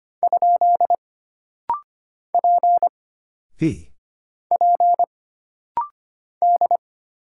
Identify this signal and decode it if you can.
Morse code, heard as a keyed tone, about 25 wpm, 700 Hz. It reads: ? P P D
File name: Single Character (Letter:Number:Punctuation:Prosign) - Rapid-Fire - 25wpm